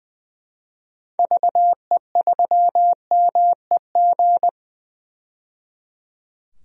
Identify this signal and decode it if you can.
Morse code, heard as a keyed tone, about 20 wpm, 700 Hz. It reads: VE3MEG
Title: DXCC-100 Call Signs - 6 Characters - 20wpm